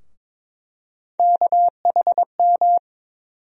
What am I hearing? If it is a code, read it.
Morse code, heard as a keyed tone, about 22 wpm, 700 Hz. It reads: KHM
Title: USA Call Sign Postfixes - 3 Letters 22wpm